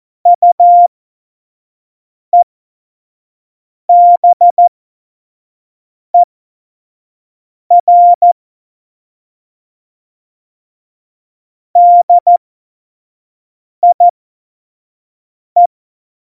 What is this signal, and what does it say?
Morse code, heard as a keyed tone, about 14 wpm, spaced as for 5 wpm, 700 Hz. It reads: UEBER DIE
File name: Morsekurs / Witze / Witz 26. September 17, Tempo 25